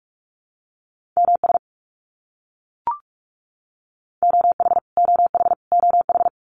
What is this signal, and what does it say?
Morse code, heard as a keyed tone, about 44 wpm, 700 Hz. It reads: MS OH OH OH
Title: US States Repeated 3x 45wpm